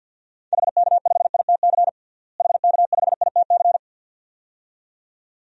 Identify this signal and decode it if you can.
Morse code, heard as a keyed tone, about 52 wpm, 700 Hz. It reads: HK5ITC HK5ITC
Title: DXCC-100 Call Signs - 6 Characters - Repeated 2x - 50wpm